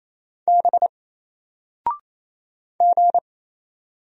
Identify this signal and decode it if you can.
Morse code, heard as a keyed tone, about 28 wpm, 700 Hz. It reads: B G
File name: Single-Letter-Number-28wpm